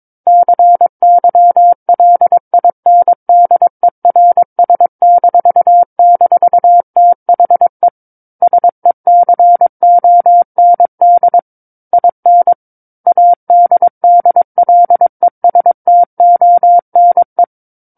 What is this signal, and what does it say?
Morse code, heard as a keyed tone, about 22 wpm, 700 Hz. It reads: CYLINDERS--THE SECOND IN ADDLESTONE
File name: Book One / 13-Chapter 13 - 22 WPM